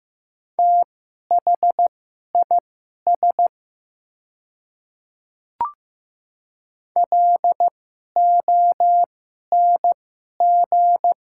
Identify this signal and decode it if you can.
Morse code, heard as a keyed tone, about 15 wpm, spaced as for 11 wpm, 700 Hz. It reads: THIS LONG
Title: 2x Character Spacing - Top 100 Words - 15wpm